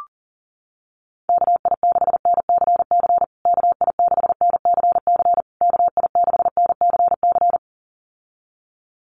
Morse code, heard as a keyed tone, about 40 wpm, 700 Hz. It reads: KI6NCC KI6NCC KI6NCC
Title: USA Call Signs Repeated 3x 40wpm